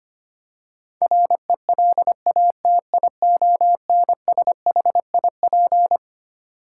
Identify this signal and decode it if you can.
Morse code, heard as a keyed tone, about 25 wpm, 700 Hz. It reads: RELATIONSHIP
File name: Twelve Letter Words - 25wpm